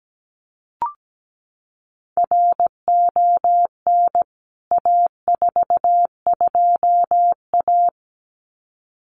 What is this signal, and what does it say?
Morse code, heard as a keyed tone, about 17 wpm, 700 Hz. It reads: RON A42A
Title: CWT Exchange 17wpm